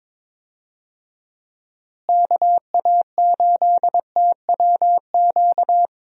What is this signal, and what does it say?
Morse code, heard as a keyed tone, about 22 wpm, 700 Hz. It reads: KA8TWQ
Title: USA Call Signs - 6 Characters - Repeated 3x - 22wpm